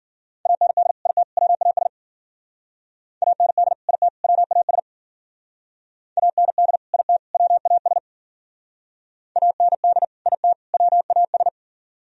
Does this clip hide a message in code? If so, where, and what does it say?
Yes: Morse code from 0.5 to 11.5 s, heard as a keyed tone, about 52 wpm, 700 Hz. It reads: AND IT WAS AND IT WAS AND IT WAS AND IT WAS